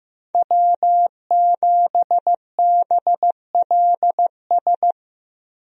Morse code, heard as a keyed tone, about 15 wpm, 700 Hz. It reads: W7BLS
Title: USA Call Signs - 5 Characters - 15wpm